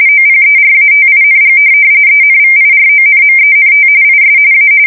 RTTY, about 45.45 baud. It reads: OPERATING ENVIRONMENT IS A
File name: fsk170 / teletype-corp